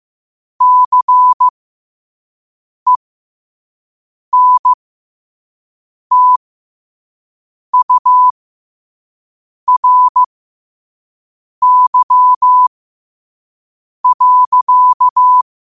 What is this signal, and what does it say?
Morse code, heard as a keyed tone, about 15 wpm, spaced as for 5 wpm, 1000 Hz. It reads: CENTURY.
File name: qotdfiles / 5wpm